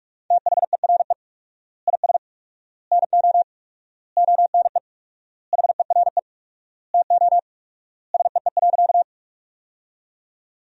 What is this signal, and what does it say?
Morse code, heard as a keyed tone, about 47 wpm, 700 Hz. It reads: THERE IS NO ONE HERE TO SEE.
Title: 4x Word-Spacing - Sentences from Top 200 Words - 45wpm